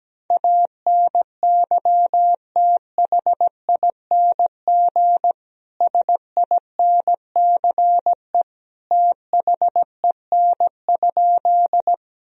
Morse code, heard as a keyed tone, about 17 wpm, 700 Hz. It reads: ANYTHING SINCE THEN?
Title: Sentences from Top 1000 Words - Review Part 1 - 17wpm